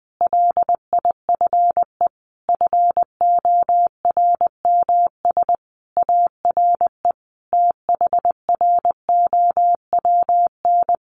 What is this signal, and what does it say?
Morse code, heard as a keyed tone, about 19 wpm, 700 Hz. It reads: LIFE FORMS ARE THROWN
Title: A Sound Of Thunder 20wpm